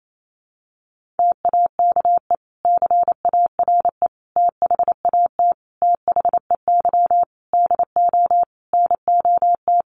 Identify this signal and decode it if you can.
Morse code, heard as a keyed tone, about 28 wpm, 700 Hz. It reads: TAKE CARE THAT THEY DO NOT
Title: Sentences from Top 1500 Words 28wpm